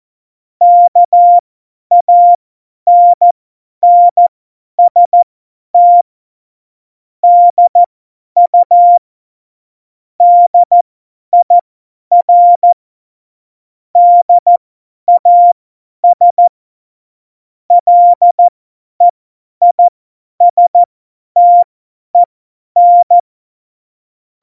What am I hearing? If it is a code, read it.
Morse code, heard as a keyed tone, about 14 wpm, spaced as for 10 wpm, 700 Hz. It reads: KANNST DU DIR DAS LEISTEN
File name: Morsekurs / Witze / Witz 07 Juli 2020, 50 F70